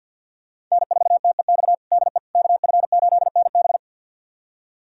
Morse code, heard as a keyed tone, about 46 wpm, 700 Hz. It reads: N4TEX DE KF8NB
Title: USA Callsign DE Callsign - 5 Characters - 50wpm